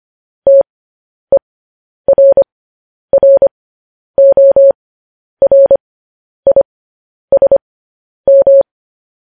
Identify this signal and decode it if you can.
Morse code, heard as a keyed tone, about 25 wpm, spaced as for 10 wpm, 550 Hz. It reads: TERRORISM
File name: podcast / News headlines-1x-25x10